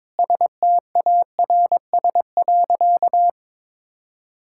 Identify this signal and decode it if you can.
Morse code, heard as a keyed tone, about 21 wpm, 700 Hz. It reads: STARS.
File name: practice / 22wpm / Sentences with 2-Word Phrases - 22wpm